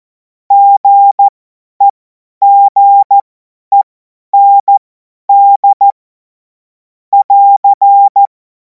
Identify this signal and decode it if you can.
Morse code, heard as a keyed tone, about 14 wpm, spaced as for 10 wpm, 800 Hz. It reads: GEGEND <AR>
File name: Morsekurs / Witze / Witz 10. März 2026 50 F70